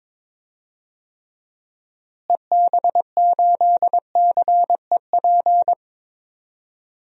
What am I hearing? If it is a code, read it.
Morse code, heard as a keyed tone, about 22 wpm, 700 Hz. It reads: EB8CEP
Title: DXCC-100 Call Signs - 6 Characters - Repeated 3x - 22wpm